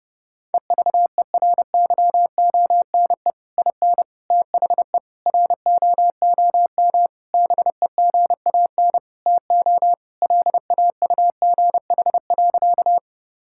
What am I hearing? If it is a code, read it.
Morse code, heard as a keyed tone, about 32 wpm, 700 Hz. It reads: EVERYONE IN THE ROOM BEGAN TO LAUGH.
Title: Sentences from Top 1000 Words - Review Part 5 - 30wpm